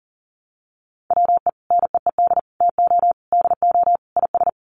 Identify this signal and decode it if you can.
Morse code, heard as a keyed tone, about 40 wpm, 700 Hz. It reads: WE NEED TO DO IS
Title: Sets of 5 Words Encore2 40wpm